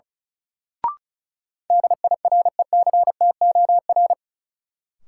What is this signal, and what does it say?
Morse code, heard as a keyed tone, about 34 wpm, 700 Hz. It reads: DIRECTOR
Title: Sentences from Top 1300 Words - New Word Drill Alt - 35wpm